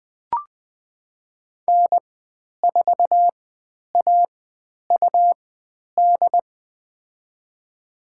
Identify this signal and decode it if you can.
Morse code, heard as a keyed tone, about 20 wpm, spaced as for 10 wpm, 700 Hz. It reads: N4AUD
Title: 102 Lesson 17g - Letter u - Call Signs 20wpm